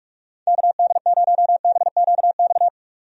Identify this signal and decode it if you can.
Morse code, heard as a keyed tone, about 45 wpm, 700 Hz. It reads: KD0BQX
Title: USA Call Signs - 6 Characters - 45wpm